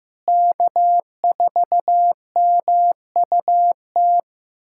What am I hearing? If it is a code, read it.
Morse code, heard as a keyed tone, about 16 wpm, 700 Hz. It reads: K4MUT
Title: USA Callsign DE Callsign - 5 Characters - 15wpm